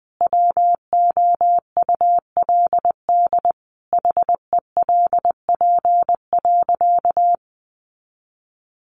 Morse code, heard as a keyed tone, about 20 wpm, 700 Hz. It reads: WOULD HELP.
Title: Sentences from Top 200 Words 20wpm